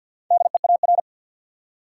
Morse code, heard as a keyed tone, about 53 wpm, 700 Hz. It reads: DEAR
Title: Four Letter Words - 50wpm